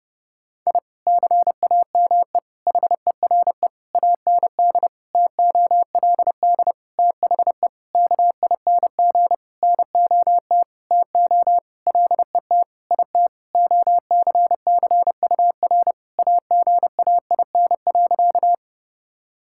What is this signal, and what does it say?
Morse code, heard as a keyed tone, about 30 wpm, 700 Hz. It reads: I CAME HERE AND TOLD THE KING NOT TO LET IT OCCUR AGAIN.
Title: Sentences from Top 1000 Words - Review Part 6 - 30wpm